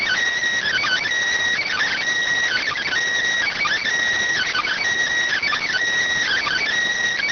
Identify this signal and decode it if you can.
Emergency Alert System (SAME) tones (header bursts), heard as harsh buzzing bursts. It is heard from 0.6 to 6.6 s.